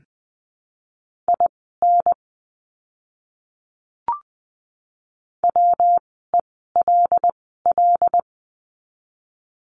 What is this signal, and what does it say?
Morse code, heard as a keyed tone, about 20 wpm, spaced as for 14 wpm, 700 Hz. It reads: IN WELL
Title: 139 Lesson 23f - Letter f - Words 20wpm